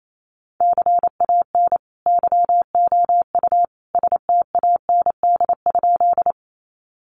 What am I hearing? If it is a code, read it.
Morse code, heard as a keyed tone, about 28 wpm, 700 Hz. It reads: CAN YOU STAND?
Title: Sentences from Top 300 Words 28wpm